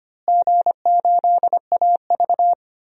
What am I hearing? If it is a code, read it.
Morse code, heard as a keyed tone, about 25 wpm, 700 Hz. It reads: G8AV
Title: DXCC-100 Call Signs - 4 Characters - Repeated 2x - 25wpm